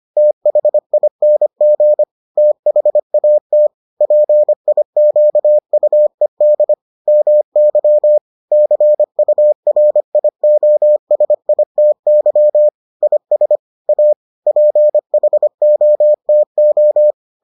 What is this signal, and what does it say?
Morse code, heard as a keyed tone, about 25 wpm, 600 Hz. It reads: THING THAT PIQUED MY CURIOSITY IS A PHOTO